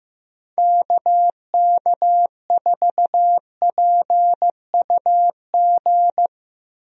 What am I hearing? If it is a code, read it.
Morse code, heard as a keyed tone, about 15 wpm, 700 Hz. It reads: KK4PUG
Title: USA Call Signs - 6 Characters - Repeated 2x - 15wpm